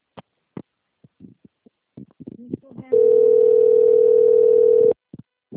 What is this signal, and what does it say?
Ringback tone (North America), 440 + 480 Hz.
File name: out- - - - - .27214